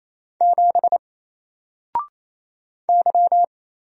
Morse code, heard as a keyed tone, about 27 wpm, 700 Hz. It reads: 7 Y
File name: Single-Letter-Number-28wpm